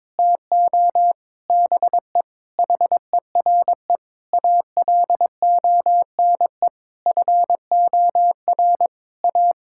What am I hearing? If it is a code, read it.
Morse code, heard as a keyed tone, about 22 wpm, 700 Hz. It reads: TO BE HERE ALONE FOR A
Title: Sentences from Top 1000 Words - Review Part 5 - 22wpm